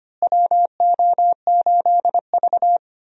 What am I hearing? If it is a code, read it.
Morse code, heard as a keyed tone, about 25 wpm, 700 Hz. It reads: WO8V
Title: USA Call Signs - 4 Characters - Repeated 2x - 25wpm